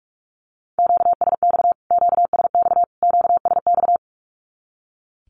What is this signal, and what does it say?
Morse code, heard as a keyed tone, about 44 wpm, 700 Hz. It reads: QSX QSX QSX
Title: Q-Codes Repeated 3x 45wpm